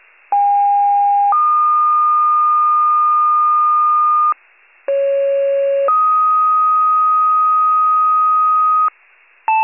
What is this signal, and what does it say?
Two-tone page, heard as a one-second tone, then a three-second tone, about 810 then 1220 Hz, about 550 then 1250 Hz. It is heard from 0.3 to 8.9 s.
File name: pager / 73 RESCUE / wcalert